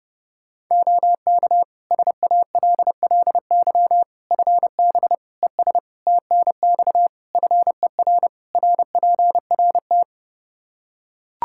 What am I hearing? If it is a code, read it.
Morse code, heard as a keyed tone, about 30 wpm, 700 Hz. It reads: OK SALLY FB ES TNX FER RPRT
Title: Practice QSO - Exchange 2 - Info - 30wpm